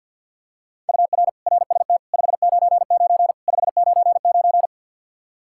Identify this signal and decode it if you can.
Morse code, heard as a keyed tone, about 50 wpm, 700 Hz. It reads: UR RST 599 599
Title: Practice QSO - Exchange 1 - Beginning - 50wpm